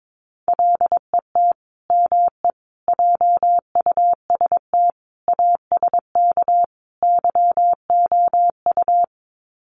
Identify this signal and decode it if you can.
Morse code, heard as a keyed tone, about 22 wpm, 700 Hz. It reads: LET ME JUST ASK YOU
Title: Sets of 5 Words Encore2 22wpm